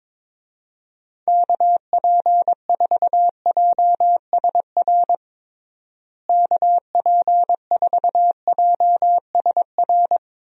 Morse code, heard as a keyed tone, about 22 wpm, 700 Hz. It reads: KP4JSR KP4JSR
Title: USA Call Signs - 6 Characters - Repeated 2x - 22wpm